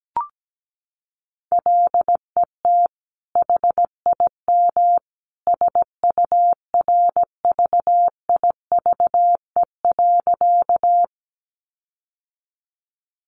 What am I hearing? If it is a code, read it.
Morse code, heard as a keyed tone, about 17 wpm, 700 Hz. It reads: LET HIM SURVIVE.
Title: Sentences from Top 1300 Words 17wpm